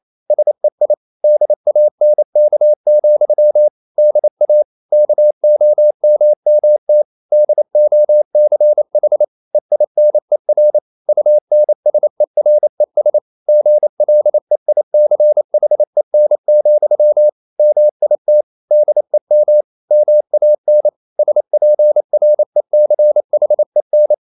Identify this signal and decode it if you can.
Morse code, heard as a keyed tone, about 28 wpm, 600 Hz. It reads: SEI DANK, DA KOMMT DOCH EINER UNSERES GLEICHEN, MIT DEM MAN SPRECHEN
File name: Liebespaar4 28wpm